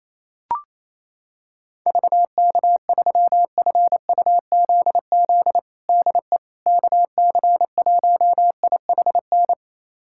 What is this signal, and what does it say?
Morse code, heard as a keyed tone, about 28 wpm, 700 Hz. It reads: VK3FUZZ DE KC1IHN
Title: USA Callsign DE Callsign 28wpm